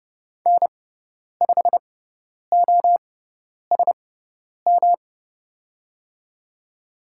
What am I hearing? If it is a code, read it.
Morse code, heard as a keyed tone, about 30 wpm, spaced as for 10 wpm, 700 Hz. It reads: N5OSM
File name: Lesson 18g - Letter m - Call Signs 30wpm